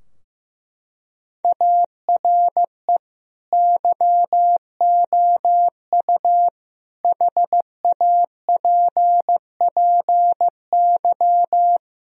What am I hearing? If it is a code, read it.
Morse code, heard as a keyed tone, about 16 wpm, 700 Hz. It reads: ARE YOU HAPPY
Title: Sentences from Top 1000 Words - Review Part 2 - 15wpm